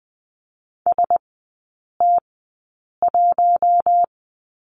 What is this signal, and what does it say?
Morse code, heard as a keyed tone, about 20 wpm, spaced as for 8 wpm, 700 Hz. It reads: ST1
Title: 100 Lesson 17e - Letter u - Three Characters 20wpm